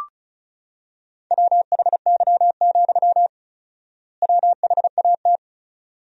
Morse code, heard as a keyed tone, about 34 wpm, 700 Hz. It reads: WHY, WHAT
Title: 4x Word-Spacing - Sentences from Top 200 Words - 35wpm